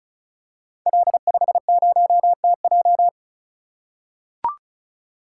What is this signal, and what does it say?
Morse code, heard as a keyed tone, about 35 wpm, 700 Hz. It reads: L50TJ
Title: Code Groups - 5 Letter:Numbers - 35wpm